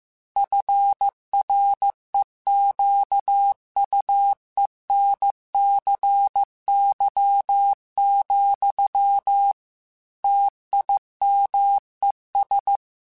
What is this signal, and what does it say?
Morse code, heard as a keyed tone, about 15 wpm, 800 Hz. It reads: FREQUENCY, TIMES
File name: Technical Subjects / Technical Subjects-mp3s / XL @13